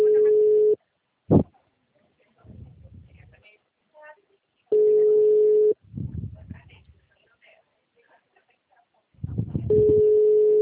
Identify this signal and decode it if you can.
Ringback tone (Europe, 425 Hz), with speech.